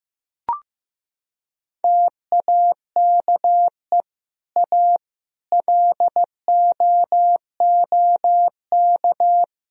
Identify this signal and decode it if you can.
Morse code, heard as a keyed tone, about 15 wpm, 700 Hz. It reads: TAKE A LOOK
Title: Sentences from Top 1000 Words - Review Part 3 - 15wpm